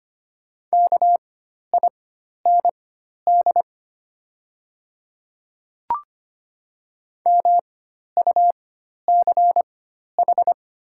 Morse code, heard as a keyed tone, about 25 wpm, spaced as for 12 wpm, 700 Hz. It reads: KIND MUCH
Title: 4x Character Spacing - Top 200 Words - 25wpm